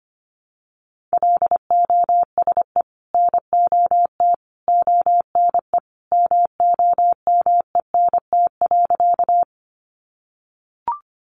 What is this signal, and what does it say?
Morse code, heard as a keyed tone, about 25 wpm, 700 Hz. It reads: LOSE NOT ONE MOMENT.
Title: Sentences from Top 1100 Words 25wpm